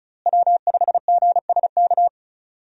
Morse code, heard as a keyed tone, about 34 wpm, 700 Hz. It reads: W5GSK
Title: USA Call Signs - 5 Characters - Repeated 2x - 35wpm